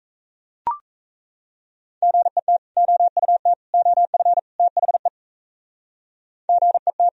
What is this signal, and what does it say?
Morse code, heard as a keyed tone, about 34 wpm, 700 Hz. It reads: GET OUT OF THE GET
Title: Speed-Racing - Sets of 4 Words 28wpm